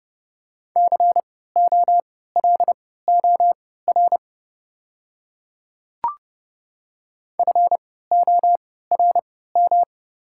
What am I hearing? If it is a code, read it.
Morse code, heard as a keyed tone, about 30 wpm, spaced as for 17 wpm, 690 Hz. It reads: COLOR FORM
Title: 3x Character Spacing - Top 300 Words - 30wpm